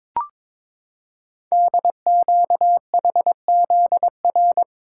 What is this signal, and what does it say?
Morse code, heard as a keyed tone, about 22 wpm, 690 Hz. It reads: DQHZR
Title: Code Groups - 5 Letter:Numbers - 22wpm